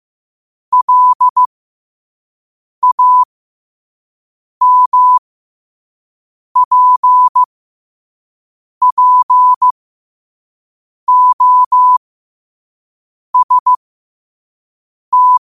Morse code, heard as a keyed tone, about 15 wpm, spaced as for 5 wpm, 1000 Hz. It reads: LAMPPOST
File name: qotdfiles / 5wpm